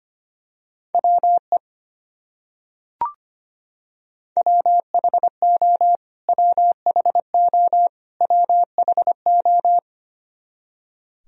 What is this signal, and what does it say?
Morse code, heard as a keyed tone, about 25 wpm, 700 Hz. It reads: WE WHO WHO WHO
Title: Top 100 Words Repeated 25wpm